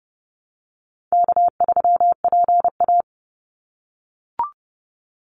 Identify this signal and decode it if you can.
Morse code, heard as a keyed tone, about 30 wpm, 700 Hz. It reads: K3PA
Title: Easy Call Signs - 30wpm